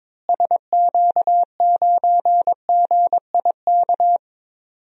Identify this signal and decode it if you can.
Morse code, heard as a keyed tone, about 22 wpm, 700 Hz. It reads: SQ9GIK
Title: DXCC-100 Call Signs - 6 Characters - Repeated 3x - 22wpm